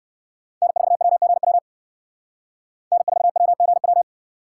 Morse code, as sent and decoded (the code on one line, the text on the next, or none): -. ....- .-- --. .-- / -. ....- .-- --. .--
N4WGW N4WGW